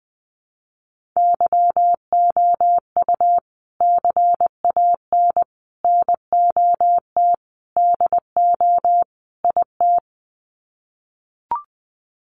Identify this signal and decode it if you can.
Morse code, heard as a keyed tone, about 20 wpm, 700 Hz. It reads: YOU CAN NOT DO IT
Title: Sets of 5 Words Encore2 20wpm